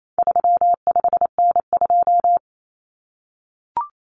Morse code, heard as a keyed tone, about 28 wpm, 700 Hz. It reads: 35N2
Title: Common Cut Numbers - 4-Digit 28wpm